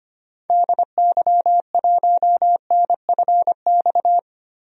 Morse code, heard as a keyed tone, about 25 wpm, 700 Hz. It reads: DY1NFX
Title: DXCC-100 Call Signs - 6 Characters - 25wpm